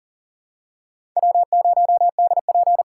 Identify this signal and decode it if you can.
Morse code, heard as a keyed tone, about 39 wpm, 700 Hz. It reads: W0DP
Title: USA Call Signs - 5 Characters - 8x Character Spacing - 40wpm